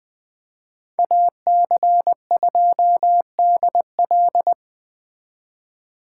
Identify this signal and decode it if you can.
Morse code, heard as a keyed tone, about 24 wpm, 700 Hz. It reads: AC2DL
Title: Speed Racing - USA Call Signs 20wpm